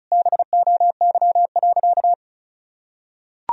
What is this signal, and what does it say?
Morse code, heard as a keyed tone, about 36 wpm, 700 Hz. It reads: BOY.
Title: Sentences from Top 1000 Words - Review Part 1 - 35wpm